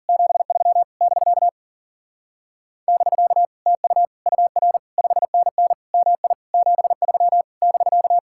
Morse code, heard as a keyed tone, about 48 wpm, 700 Hz. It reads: 73 <BK> <BK> TU UR 5NN MI 73 <BK>
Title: Speed-Racing Common QSO Phrases - 40wpm